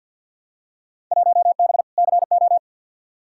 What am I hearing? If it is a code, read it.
Morse code, heard as a keyed tone, about 52 wpm, 700 Hz. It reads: 1B CO
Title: Field Day Exchanges - 50wpm